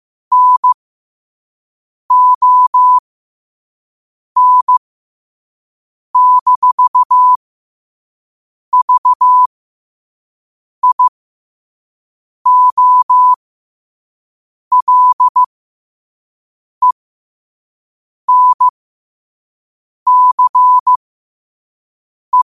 Morse code, heard as a keyed tone, about 15 wpm, spaced as for 5 wpm, 1000 Hz. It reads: NON-VIOLENCE
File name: qotdfiles / 5wpm